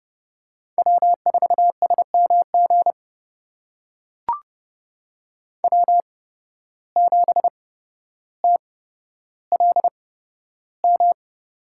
Morse code, heard as a keyed tone, about 30 wpm, spaced as for 13 wpm, 700 Hz. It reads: W4SMG W7TLM
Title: USA Call Signs - 5 Characters - 8x Character Spacing - 30wpm